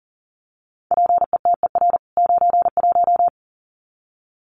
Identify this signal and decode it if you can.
Morse code, heard as a keyed tone, about 40 wpm, 700 Hz. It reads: PETER 91
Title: CWT Exchange 40wpm